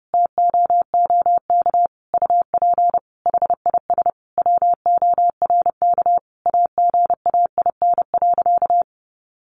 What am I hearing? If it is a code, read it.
Morse code, heard as a keyed tone, about 28 wpm, 700 Hz. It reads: TOOK UP HIS WORK AGAIN.
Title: Sentences from Top 1500 Words 30wpm